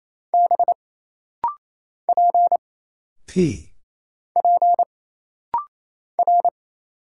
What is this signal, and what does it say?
Morse code, heard as a keyed tone, about 27 wpm, 700 Hz. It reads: B P P R
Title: Single Letters - Rapid-Fire 28wpm